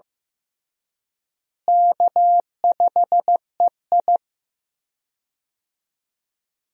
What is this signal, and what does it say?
Morse code, heard as a keyed tone, about 15 wpm, 700 Hz. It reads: K5EI
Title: USA Call Signs - 4 Characters - Repeated 3x - 15wpm